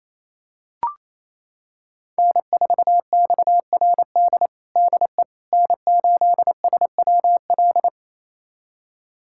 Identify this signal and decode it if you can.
Morse code, heard as a keyed tone, about 28 wpm, 700 Hz. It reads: N4XRD DE N8SWL
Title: USA Callsign DE Callsign 28wpm